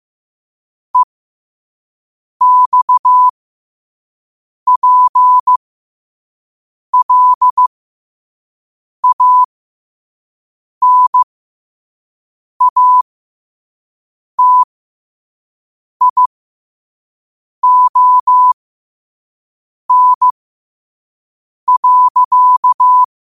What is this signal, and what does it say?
Morse code, heard as a keyed tone, about 15 wpm, spaced as for 5 wpm, 1000 Hz. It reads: EXPLANATION.
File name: qotdfiles / 5wpm